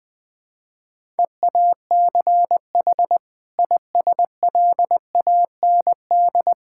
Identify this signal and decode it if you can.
Morse code, heard as a keyed tone, about 20 wpm, 700 Hz. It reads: EACH ISLAND
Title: Sentences from Top 1000 Words - Review Part 6 - 20wpm